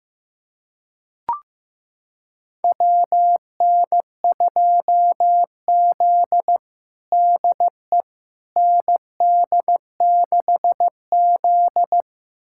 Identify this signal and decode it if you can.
Morse code, heard as a keyed tone, about 15 wpm, 700 Hz. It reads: WN2Z DE ND6Z
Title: USA Callsign DE Callsign - 4 Characters - 15wpm